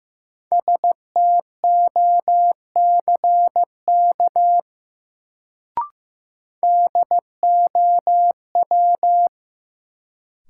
Morse code, heard as a keyed tone, about 15 wpm, 700 Hz. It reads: STOCK DOW
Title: Sets of 5 Words - New Word Drill - 15wpm